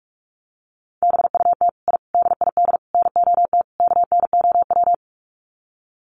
Morse code, heard as a keyed tone, about 44 wpm, 700 Hz. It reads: BUT I DID NOT KNOW
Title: Sets of 5 Words Encore 45wpm